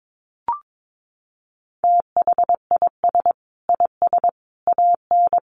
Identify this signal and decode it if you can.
Morse code, heard as a keyed tone, about 22 wpm, 700 Hz. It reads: THIS IS AN
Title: Sentences from Top 900 Words 22wpm